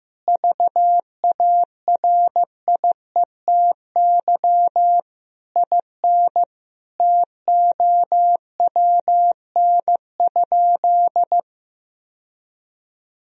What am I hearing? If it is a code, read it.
Morse code, heard as a keyed tone, about 15 wpm, 700 Hz. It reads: VARIETY IN TOWN?
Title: Sentences from Top 1000 Words - Review Part 4 - 15wpm